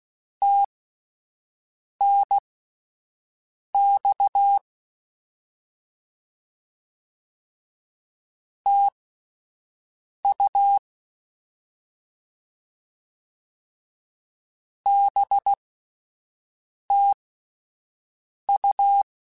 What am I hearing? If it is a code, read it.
Morse code, heard as a keyed tone, about 16 wpm, spaced as for 5 wpm, 800 Hz. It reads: TNX TU BTU
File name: Misc / Misc-mp3s / Words @5